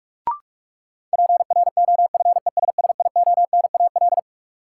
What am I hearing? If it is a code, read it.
Morse code, heard as a keyed tone, about 46 wpm, 700 Hz. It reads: PROFESSIONAL
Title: Sentences from Top 1000 Words - Review Part 4 - New Word Drill - 45wpm